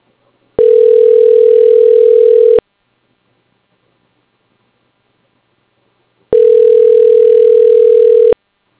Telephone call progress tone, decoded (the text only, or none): ringback tone